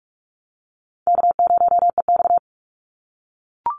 Morse code, heard as a keyed tone, about 44 wpm, 700 Hz. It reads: K0EX
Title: Easy Call Signs - 45wpm